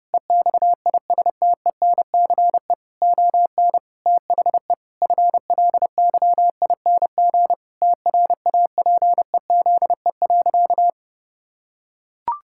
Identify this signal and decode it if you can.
Morse code, heard as a keyed tone, about 30 wpm, 700 Hz. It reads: EXISTENCE ON THE FLYING TRAPEZE.
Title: Pangram Challenge v2 - 30wpm